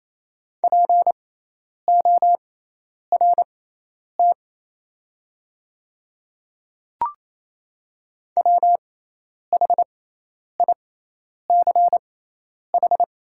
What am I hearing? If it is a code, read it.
Morse code, heard as a keyed tone, about 27 wpm, spaced as for 10 wpm, 700 Hz. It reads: PORT WHICH
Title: 6x Character Spacing - Top 200 Words - 28wpm